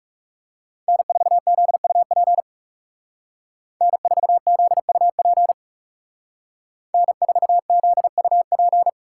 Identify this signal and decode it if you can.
Morse code, heard as a keyed tone, about 37 wpm, 700 Hz. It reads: N4ZUP N4ZUP N4ZUP
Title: Speed Racing - USA 1x3 Call Signs 30wpm